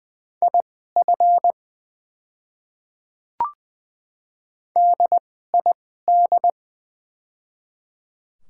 Morse code, heard as a keyed tone, about 20 wpm, spaced as for 15 wpm, 700 Hz. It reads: IF DID
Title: 2x Character Spacing - Top 200 Words - 20wpm